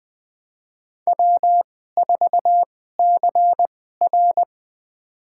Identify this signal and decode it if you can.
Morse code, heard as a keyed tone, about 20 wpm, spaced as for 14 wpm, 700 Hz. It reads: W4CR
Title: USA Call Signs - 2x Character Spacing - 20wpm